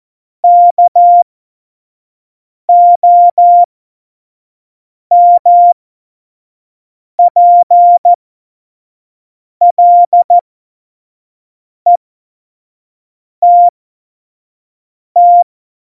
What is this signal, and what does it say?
Morse code, heard as a keyed tone, about 14 wpm, spaced as for 5 wpm, 700 Hz. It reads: KOMPLETT
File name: Morsekurs / Witze / Witz 25. Juni 2019, 25 F70